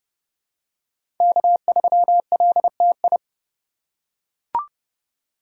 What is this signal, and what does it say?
Morse code, heard as a keyed tone, about 31 wpm, 700 Hz. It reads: K3LTI
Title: USA Call Signs - 5 Characters - 2x Character Spacing - 30wpm